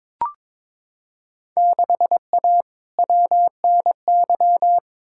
Morse code, heard as a keyed tone, about 22 wpm, 700 Hz. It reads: 6A WNY
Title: Field Day Exchanges Encore - 22wpm